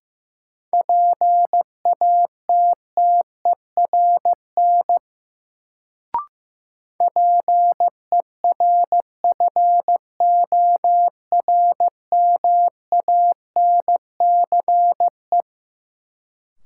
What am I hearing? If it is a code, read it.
Morse code, heard as a keyed tone, about 15 wpm, 700 Hz. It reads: PATTERN PERFORMANCE
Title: Sentences from Top 1000 Words - New Word Drill - 15wpm